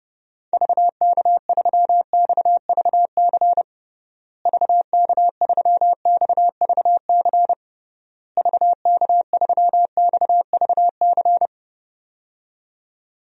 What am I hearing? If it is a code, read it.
Morse code, heard as a keyed tone, about 30 wpm, 700 Hz. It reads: VK3XVC VK3XVC VK3XVC
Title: DXCC-100 Call Signs - 6 Characters - Repeated 3x - 30wpm